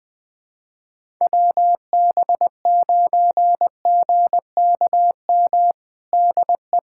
Morse code, heard as a keyed tone, about 21 wpm, 700 Hz. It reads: WB9GKM DE
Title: USA Callsign DE Callsign - 6 Characters - 20wpm